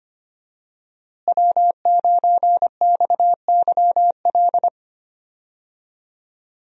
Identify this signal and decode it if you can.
Morse code, heard as a keyed tone, about 25 wpm, 700 Hz. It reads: W9XYL
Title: USA Call Signs - 5 Characters - Repeated 2x - 25wpm